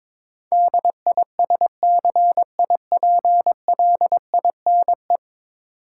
Morse code, heard as a keyed tone, about 21 wpm, 700 Hz. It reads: DISCIPLINE
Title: Sentences from Top 900 Words - New Word Drill - 22wpm